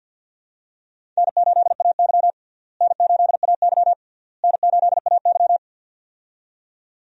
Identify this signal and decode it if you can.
Morse code, heard as a keyed tone, about 52 wpm, 700 Hz. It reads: N8AY N8AY N8AY
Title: USA Call Signs - 4 Characters - Repeated 3x - 50wpm